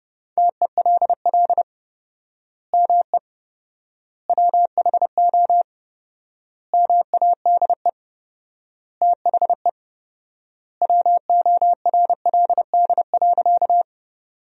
Morse code, heard as a keyed tone, about 30 wpm, 700 Hz. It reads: TELL ME WHO MADE THE WORLD.
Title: 4x Word-Spacing - Sentences from Top 200 Words - 30wpm